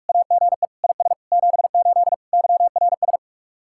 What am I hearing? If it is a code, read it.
Morse code, heard as a keyed tone, about 46 wpm, 700 Hz. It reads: AGE IS 78 YRS